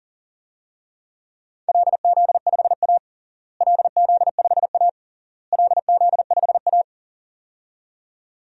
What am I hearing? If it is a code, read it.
Morse code, heard as a keyed tone, about 40 wpm, 700 Hz. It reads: LZ5A LZ5A LZ5A
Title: DXCC-100 Call Signs - 4 Characters - Repeated 3x - 40wpm